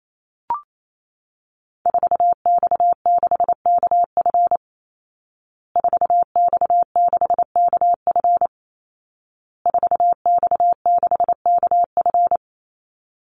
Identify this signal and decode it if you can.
Morse code, heard as a keyed tone, about 28 wpm, 700 Hz. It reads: 4X6KF 4X6KF 4X6KF
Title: Call Signs Repeated 2x - 28wpm